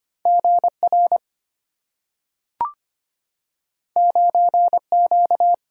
Morse code, heard as a keyed tone, about 25 wpm, 700 Hz. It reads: G R 9 Q
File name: Two Letter-Number Combinations - 25wpm